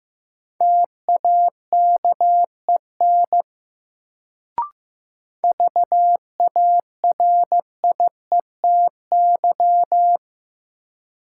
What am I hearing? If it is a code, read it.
Morse code, heard as a keyed tone, about 15 wpm, 700 Hz. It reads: TAKEN VARIETY
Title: Sentences from Top 1000 Words - Review Part 4 - New Word Drill - 15wpm